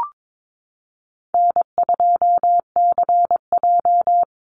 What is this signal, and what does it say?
Morse code, heard as a keyed tone, about 22 wpm, 700 Hz. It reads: N2CJ
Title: Easy Call Signs Repeated 3x - 22wpm